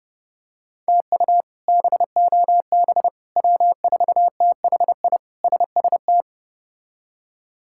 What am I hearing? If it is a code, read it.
Morse code, heard as a keyed tone, about 30 wpm, 700 Hz. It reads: TU BOB W4THI SST
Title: K1USN SST QSO Practice - 30wpm